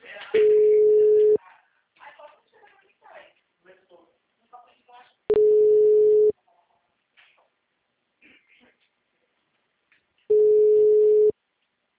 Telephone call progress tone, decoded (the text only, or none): ringback tone